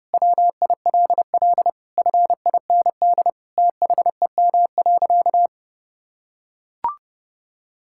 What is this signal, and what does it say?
Morse code, heard as a keyed tone, about 30 wpm, 700 Hz. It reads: WILL FIND THEM.
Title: Sentences from Top 1000 Words - Review Part 5 - 30wpm